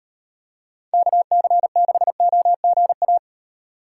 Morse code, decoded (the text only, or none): KC6OGA